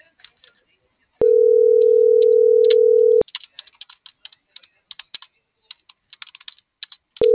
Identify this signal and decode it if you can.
Ringback tone (North America), 440 + 480 Hz.